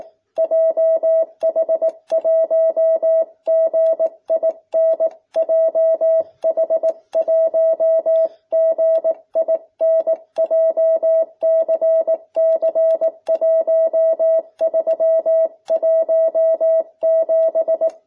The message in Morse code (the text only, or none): JH1GIN JH1GIN JCC1317